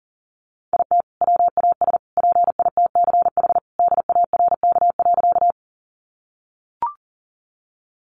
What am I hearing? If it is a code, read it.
Morse code, heard as a keyed tone, about 40 wpm, 700 Hz. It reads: IT WAS PITCH DARK.
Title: Sentences from Top 1300 Words 40wpm